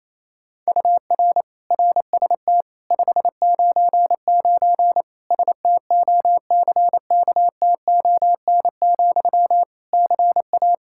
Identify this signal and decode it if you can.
Morse code, heard as a keyed tone, about 28 wpm, 700 Hz. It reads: UR RST 599 STOCKTON, CA
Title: K1USN SST QSO Practice 28wpm